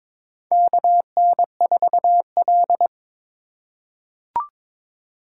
Morse code, heard as a keyed tone, about 22 wpm, 700 Hz. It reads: KN4L
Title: USA Call Signs - 4 Characters - 3x Character Spacing - 22wpm